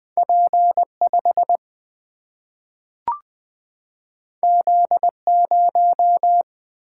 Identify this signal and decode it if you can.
Morse code, heard as a keyed tone, about 20 wpm, 700 Hz. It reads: P5 Z0
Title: Single-Letter Single-Number 20wpm